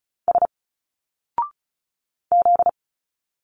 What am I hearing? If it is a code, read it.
Morse code, heard as a keyed tone, about 34 wpm, 700 Hz. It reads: S Z
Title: Single Letters 35wpm